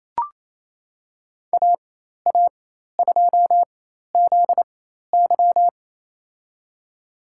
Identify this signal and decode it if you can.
Morse code, heard as a keyed tone, about 27 wpm, spaced as for 13 wpm, 700 Hz. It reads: AA2ZY
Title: USA Call Signs - 5 Characters - 4x Character Spacing - 28wpm